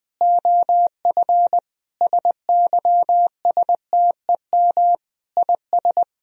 Morse code, heard as a keyed tone, about 20 wpm, 700 Hz. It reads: OF SYSTEM IS
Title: Sentences from Top 1000 Words - Review Part 3 - 20wpm